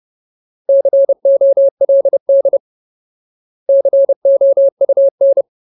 Morse code, heard as a keyed tone, about 30 wpm, 550 Hz. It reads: COLD COUN
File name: Ultimate Word List / 30WPM Ultimate